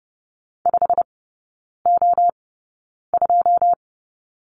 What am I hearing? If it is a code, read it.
Morse code, heard as a keyed tone, about 30 wpm, spaced as for 9 wpm, 700 Hz. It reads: 5O2
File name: Lesson 17e - Letter u - Three Characters 30wpm